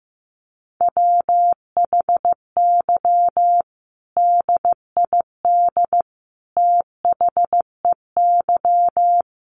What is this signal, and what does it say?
Morse code, heard as a keyed tone, about 14 wpm, 700 Hz. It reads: WHY DID THEY
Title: Sentences from Top 1500 Words 15wpm